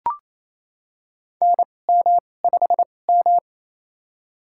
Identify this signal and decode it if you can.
Morse code, heard as a keyed tone, about 28 wpm, spaced as for 20 wpm, 700 Hz. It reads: NM5M
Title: USA Call Signs - 4 Characters - 2x Character Spacing - 28wpm